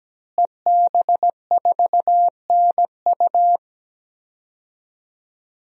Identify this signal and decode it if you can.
Morse code, heard as a keyed tone, about 17 wpm, 700 Hz. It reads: EB4NU
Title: DXCC-100 Call Signs - 5 Characters - Repeated 2x - 17wpm